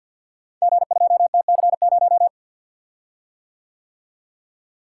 Morse code, heard as a keyed tone, about 51 wpm, 700 Hz. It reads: G2TC0
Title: Code Groups - 5 Letter:Numbers - 50wpm